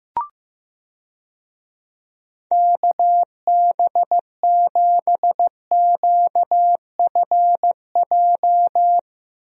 Morse code, heard as a keyed tone, about 15 wpm, 700 Hz. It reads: KB7QFJ
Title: USA Call Signs - 6 Characters - Repeated 2x - 15wpm